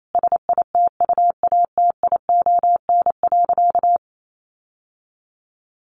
Morse code, heard as a keyed tone, about 28 wpm, 700 Hz. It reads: SITUATION.
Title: Sentences from Top 1200 Words 28wpm